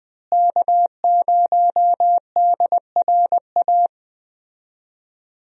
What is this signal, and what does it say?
Morse code, heard as a keyed tone, about 20 wpm, 700 Hz. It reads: K0DRA
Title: USA Call Signs - 5 Characters - Repeated 2x - 20wpm